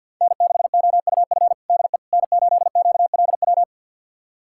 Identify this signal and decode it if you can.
Morse code, heard as a keyed tone, about 49 wpm, 700 Hz. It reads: N6OUF DE N8QLW
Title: USA Callsign DE Callsign - 5 Characters - 50wpm